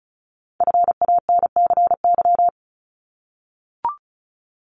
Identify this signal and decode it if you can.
Morse code, heard as a keyed tone, about 36 wpm, 700 Hz. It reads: FANCY
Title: Multi-letter Phonemes Repeated 3x 35wpm